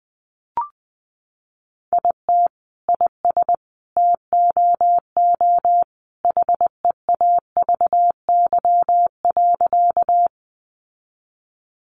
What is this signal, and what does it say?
Morse code, heard as a keyed tone, about 20 wpm, 700 Hz. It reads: IT IS TOO HEAVY.
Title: Sentences from Top 1100 Words 20wpm